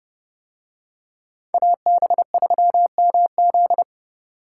Morse code, heard as a keyed tone, about 26 wpm, 700 Hz. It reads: AB3MZ